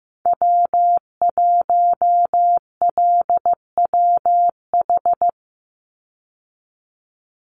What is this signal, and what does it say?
Morse code, heard as a keyed tone, about 15 wpm, 700 Hz. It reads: W1LWH
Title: Call Signs Repeated 3x - 15wpm